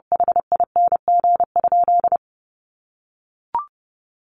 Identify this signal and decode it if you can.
Morse code, heard as a keyed tone, about 30 wpm, 700 Hz. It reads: HING?
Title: Sentences from Top 1100 Words 30wpm